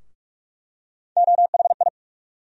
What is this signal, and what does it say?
Morse code, heard as a keyed tone, about 43 wpm, 700 Hz. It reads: OHI
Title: USA Call Sign Postfixes - 3 Letters 45wpm